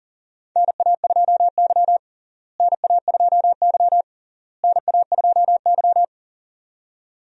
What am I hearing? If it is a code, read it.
Morse code, heard as a keyed tone, about 40 wpm, 700 Hz. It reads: NA2Y NA2Y NA2Y
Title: USA Call Signs - 4 Characters - Repeated 3x - 40wpm